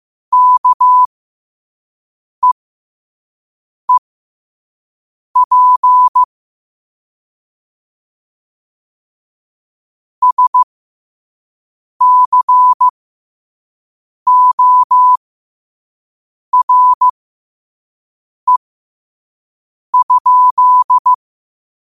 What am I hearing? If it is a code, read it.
Morse code, heard as a keyed tone, about 15 wpm, spaced as for 5 wpm, 1000 Hz. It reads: KEEP SCORE?
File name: qotdfiles / 5wpm